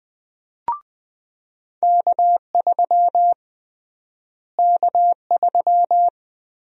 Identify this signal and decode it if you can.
Morse code, heard as a keyed tone, about 20 wpm, 700 Hz. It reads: K3 K3
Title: USA Call Sign Prefixes - Repeated 2x - 20wpm